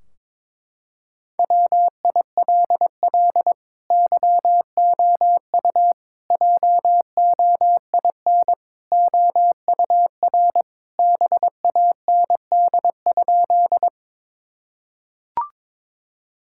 Morse code, heard as a keyed tone, about 22 wpm, 700 Hz. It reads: WILL YOU JOIN OUR BAND?
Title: Sentences from Top 1600 Words 22wpm